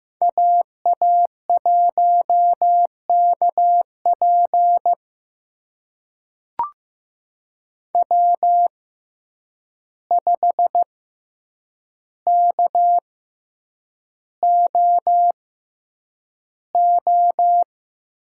Morse code, heard as a keyed tone, about 15 wpm, spaced as for 8 wpm, 700 Hz. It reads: AA1KP W5KOO
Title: USA Call Signs - 6x Character Spacing - 15wpm